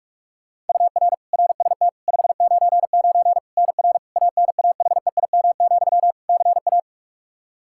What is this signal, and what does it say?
Morse code, heard as a keyed tone, about 46 wpm, 700 Hz. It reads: UR RST 599 NR ANAHEIM, CA
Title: Practice QSO - Exchange 1 - Beginning - 45wpm